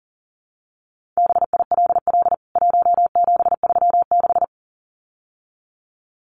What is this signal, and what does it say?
Morse code, heard as a keyed tone, about 40 wpm, 700 Hz. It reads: BILL 1736
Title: CWT Exchange 40wpm